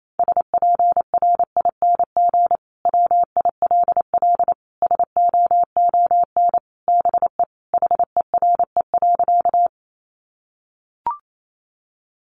Morse code, heard as a keyed tone, about 28 wpm, 700 Hz. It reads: SPRING WILL SOON BE HERE.
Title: Sentences from Top 1200 Words 28wpm